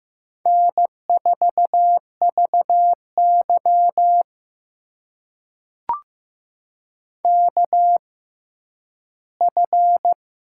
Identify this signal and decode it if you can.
Morse code, heard as a keyed tone, about 15 wpm, spaced as for 10 wpm, 700 Hz. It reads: N4VY KF
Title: USA Call Signs - 4 Characters - 6x Character Spacing - 15wpm